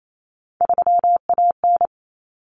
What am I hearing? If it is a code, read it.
Morse code, heard as a keyed tone, about 28 wpm, 700 Hz. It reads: 3AN
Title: Common Cut Numbers - 3-Digit 28wpm